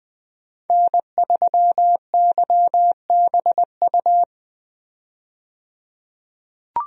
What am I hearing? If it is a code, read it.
Morse code, heard as a keyed tone, about 20 wpm, 700 Hz. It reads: N3YBU
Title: USA Call Signs - 5 Characters - 20wpm